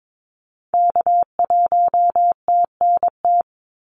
Morse code, heard as a keyed tone, about 22 wpm, 700 Hz. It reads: K1TNT
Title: NA SOTA Activator Callsigns 22wpm